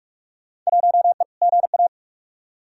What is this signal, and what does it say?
Morse code, heard as a keyed tone, about 44 wpm, 700 Hz. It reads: 1E GA
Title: Field Day Exchanges - 45wpm